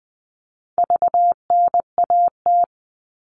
Morse code, heard as a keyed tone, about 20 wpm, 700 Hz. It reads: VNAT
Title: All Cut Numbers - 4-Digit 20wpm